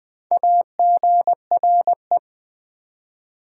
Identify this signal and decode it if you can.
Morse code, heard as a keyed tone, about 20 wpm, 700 Hz. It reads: AGRE
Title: Build A Word - 1 Letter At A Time - 5-Letter Words - 20wpm